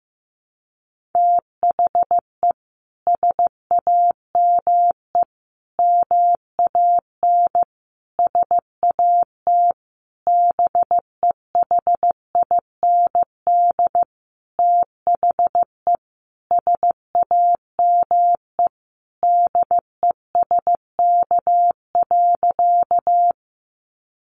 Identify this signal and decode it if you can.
Morse code, heard as a keyed tone, about 15 wpm, 700 Hz. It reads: THE SAME MAN SAT BEHIND THE SAME DESK.
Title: A Sound Of Thunder 15wpm